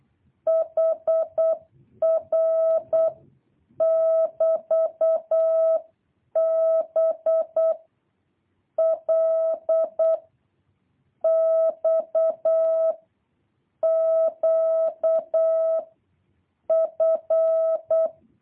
Morse code, heard as a keyed tone, about 8 wpm, 650 Hz. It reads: HR <BT> B L X Q F
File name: jh8cbh / 2023cw